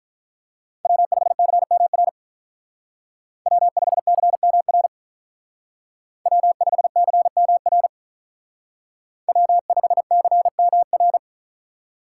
Morse code, heard as a keyed tone, about 43 wpm, 700 Hz. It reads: W5CMR W5CMR W5CMR W5CMR
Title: Speed Racing - USA 1x3 Call Signs 35wpm